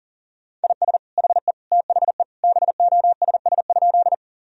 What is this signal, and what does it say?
Morse code, heard as a keyed tone, about 40 wpm, 690 Hz. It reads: IS HE THE BOSS?
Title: Sentences from Top 1000 Words - Review Part 4 - 40wpm